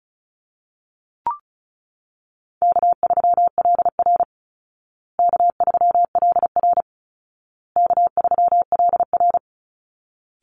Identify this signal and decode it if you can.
Morse code, heard as a keyed tone, about 34 wpm, 700 Hz. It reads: K3LR K3LR K3LR
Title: Easy Call Signs Repeated 3x - 35wpm